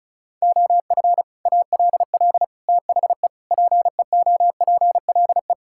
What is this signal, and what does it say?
Morse code, heard as a keyed tone, about 36 wpm, 700 Hz. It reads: OF ALL THE PEOPLE
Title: Sentences from Top 1000 Words - Review Part 6 - 35wpm